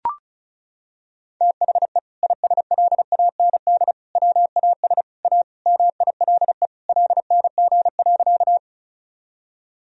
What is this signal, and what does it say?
Morse code, heard as a keyed tone, about 34 wpm, 700 Hz. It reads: THE ISLAND WAS A MILE LNG.
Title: Sentences from Top 500 Words and Abbreviations 35wpm